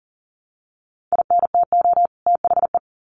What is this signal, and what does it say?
Morse code, heard as a keyed tone, about 40 wpm, 700 Hz. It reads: INTO THE
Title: Sets of 2 Words 40wpm